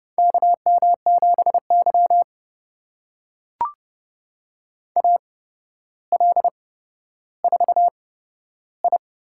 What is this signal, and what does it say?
Morse code, heard as a keyed tone, about 30 wpm, spaced as for 13 wpm, 700 Hz. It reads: KM7Y AL4I
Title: USA Call Signs - 4 Characters - 8x Character Spacing - 30wpm